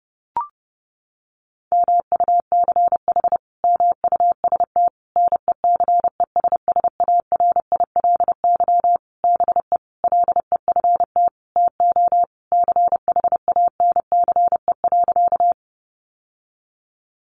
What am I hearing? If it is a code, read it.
Morse code, heard as a keyed tone, about 30 wpm, 700 Hz. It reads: MUCH MUST NECESSARILY BE LEFT TO CHANCE.
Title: Sentences from Top 1400 Words 30wpm